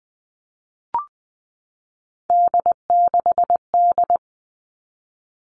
Morse code, heard as a keyed tone, about 20 wpm, 700 Hz. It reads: D6D
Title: All Cut Numbers - 3-Digit 20wpm